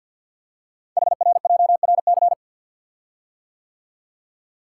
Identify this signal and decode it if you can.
Morse code, heard as a keyed tone, about 50 wpm, 700 Hz. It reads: HRJRC
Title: Code Groups - 5 Letters - 50wpm